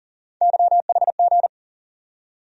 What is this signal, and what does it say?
Morse code, heard as a keyed tone, about 40 wpm, 700 Hz. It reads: YHG
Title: USA Call Sign Postfixes - 3 Letters 40wpm